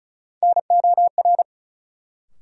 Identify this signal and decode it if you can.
Morse code, heard as a keyed tone, about 34 wpm, 700 Hz. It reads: NOR
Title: Sentences from Top 1500 Words - New Word Drill - 35wpm